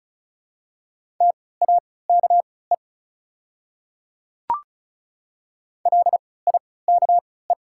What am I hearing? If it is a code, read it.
Morse code, heard as a keyed tone, about 36 wpm, spaced as for 20 wpm, 700 Hz. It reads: TAKE LIKE
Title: 3x Character Spacing - Top 200 Words - 35wpm